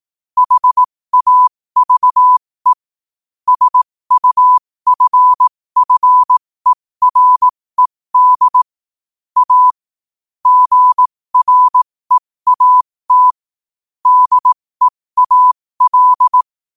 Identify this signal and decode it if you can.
Morse code, heard as a keyed tone, about 18 wpm, spaced as for 15 wpm, 1000 Hz. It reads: HAVE SUFFERED A GREAT DEAL
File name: qotdfiles / 15wpm